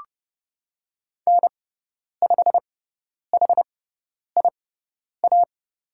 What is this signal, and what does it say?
Morse code, heard as a keyed tone, about 30 wpm, spaced as for 10 wpm, 700 Hz. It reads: N5HIA
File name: Lesson 16g - Letter c - Call Signs 30wpm